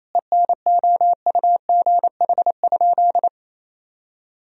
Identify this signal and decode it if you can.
Morse code, heard as a keyed tone, about 27 wpm, 700 Hz. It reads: ENOUGH?
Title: Sentences from Top 1000 Words - Review Part 4 - 28wpm